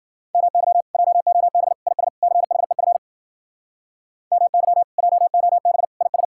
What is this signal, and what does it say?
Morse code, heard as a keyed tone, about 60 wpm, 700 Hz. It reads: MY JOB IS CHEF MY JOB IS